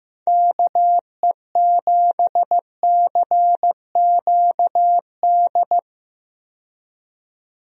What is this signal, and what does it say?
Morse code, heard as a keyed tone, about 15 wpm, 700 Hz. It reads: KE7CQD
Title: USA Call Signs - 6 Characters - 15wpm